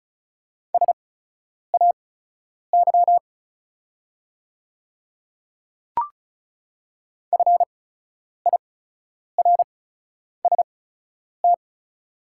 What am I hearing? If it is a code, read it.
Morse code, heard as a keyed tone, about 36 wpm, spaced as for 10 wpm, 690 Hz. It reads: SAY FIRST
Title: 8x Character Spacing - Top 200 Words - 35wpm